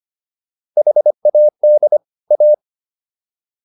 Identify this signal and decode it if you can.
Morse code, heard as a keyed tone, about 25 wpm, 600 Hz. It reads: HAD A